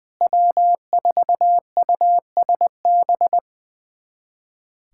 Morse code, heard as a keyed tone, about 21 wpm, 700 Hz. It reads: W4USB
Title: USA Callsign DE Callsign - 5 Characters - 20wpm